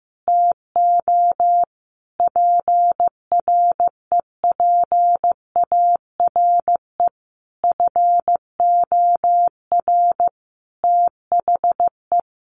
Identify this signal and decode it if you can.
Morse code, heard as a keyed tone, about 14 wpm, 700 Hz. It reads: TO PREPARE FOR THE
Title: Sentences from Top 1200 Words 15wpm